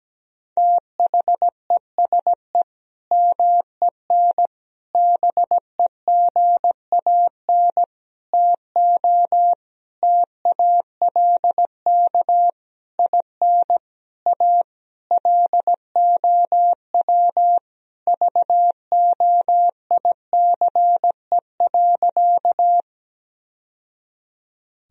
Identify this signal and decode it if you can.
Morse code, heard as a keyed tone, about 17 wpm, 700 Hz. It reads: THESE MEN BEGAN TO TALK IN A LOW VOICE.
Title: Sentences from Top 1000 Words - Review Part 1 - 17wpm